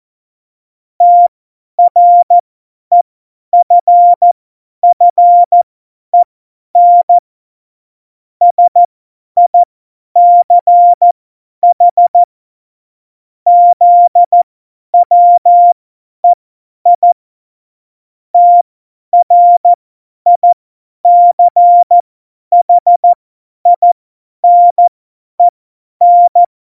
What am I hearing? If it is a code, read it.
Morse code, heard as a keyed tone, about 14 wpm, spaced as for 10 wpm, 700 Hz. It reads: TREFFEN SICH ZWEI TRICHINEN
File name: Morsekurs / Witze / Witz 25. September 2018, 50 F70